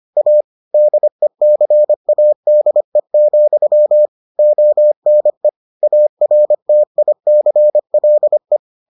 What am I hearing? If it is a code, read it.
Morse code, heard as a keyed tone, about 25 wpm, 600 Hz. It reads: A DECADE, ONE ARTICLE